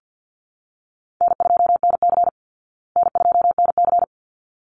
Morse code, heard as a keyed tone, about 50 wpm, 700 Hz. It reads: N2NC N2NC
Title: Easy Call Signs Repeated 2x - 50wpm